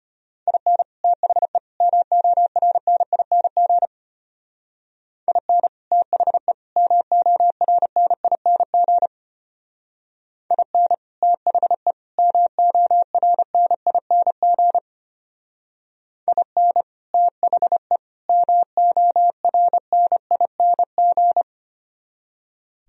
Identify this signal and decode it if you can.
Morse code, heard as a keyed tone, about 30 wpm, 700 Hz. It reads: IN THE MORNING IN THE MORNING IN THE MORNING IN THE MORNING
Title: Speed-Racing - Sets of 3 Words 25wpm